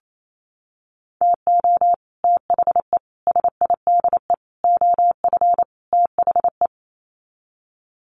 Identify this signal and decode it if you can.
Morse code, heard as a keyed tone, about 28 wpm, 700 Hz. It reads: TO THE SIDE OF THE
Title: Sets of 5 Words 28wpm